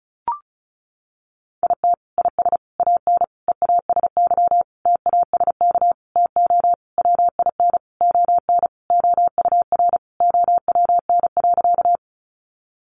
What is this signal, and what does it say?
Morse code, heard as a keyed tone, about 34 wpm, 700 Hz. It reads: IT IS AN EASY TASK TO WIN ON OUR OWN.
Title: Sentences from Top 1500 Words 35wpm